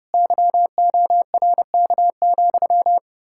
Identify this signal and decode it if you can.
Morse code, heard as a keyed tone, about 36 wpm, 700 Hz. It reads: YORK,
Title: Speed-Racing Common QSO Phrases Encore - 30wpm